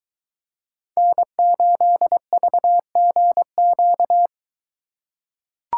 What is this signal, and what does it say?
Morse code, heard as a keyed tone, about 22 wpm, 700 Hz. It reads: N8VGQ